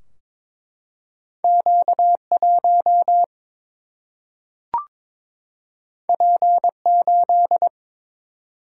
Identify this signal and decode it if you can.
Morse code, heard as a keyed tone, about 22 wpm, 700 Hz. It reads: Q 1 P 8
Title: Single-Letter Single-Number 22wpm